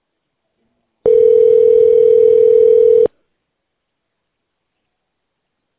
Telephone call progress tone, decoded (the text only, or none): ringback tone